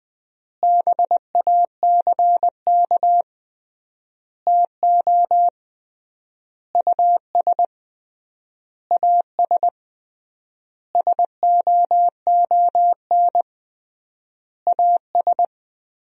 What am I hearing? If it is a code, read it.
Morse code, heard as a keyed tone, about 20 wpm, 700 Hz. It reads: BACK TO US AS SOON AS
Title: 3x Word-Spacing - Sentences from Top 500 Words - 20wpm